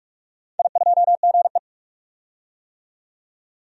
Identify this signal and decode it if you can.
Morse code, heard as a keyed tone, about 46 wpm, 700 Hz. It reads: I2GE
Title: DXCC-100 Call Signs - 4 Characters - 45wpm